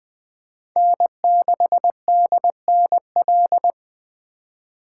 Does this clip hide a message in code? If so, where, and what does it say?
Yes: Morse code from 0.8 to 3.7 s, heard as a keyed tone, about 20 wpm, 700 Hz. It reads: N6DNL